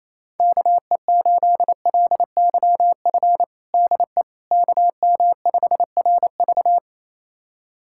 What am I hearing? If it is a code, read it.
Morse code, heard as a keyed tone, about 28 wpm, 700 Hz. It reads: KE8LYF DE KM5RV
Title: USA Callsign DE Callsign 28wpm